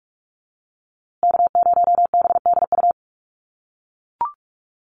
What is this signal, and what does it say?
Morse code, heard as a keyed tone, about 44 wpm, 700 Hz. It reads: K0BDU
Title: USA Call Signs Repeated 3x 45wpm